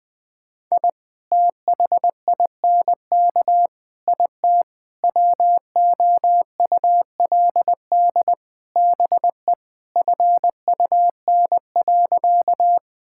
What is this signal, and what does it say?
Morse code, heard as a keyed tone, about 21 wpm, 700 Hz. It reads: I THINK IT WOULD BE FUN.
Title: Sentences with 5-Word Phrases - 20wpm